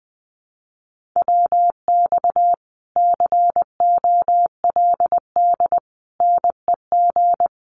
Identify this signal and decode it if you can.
Morse code, heard as a keyed tone, about 20 wpm, 700 Hz. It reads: WX COLD NEG
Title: Common QSO Phrases Encore 20wpm